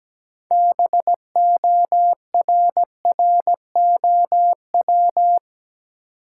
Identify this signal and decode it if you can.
Morse code, heard as a keyed tone, about 17 wpm, 700 Hz. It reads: BORROW
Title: Sentences from Top 1500 Words - New Word Drill - 17wpm